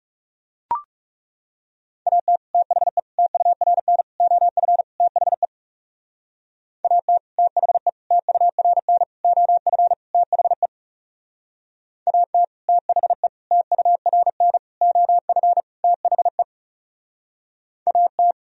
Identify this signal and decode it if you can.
Morse code, heard as a keyed tone, about 40 wpm, 700 Hz. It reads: AT THE TURN OF THE AT THE TURN OF THE AT THE TURN OF THE AT